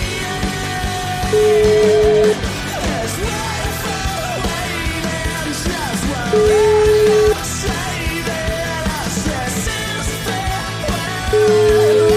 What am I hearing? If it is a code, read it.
Ringback tone (Europe, 425 Hz), with noise.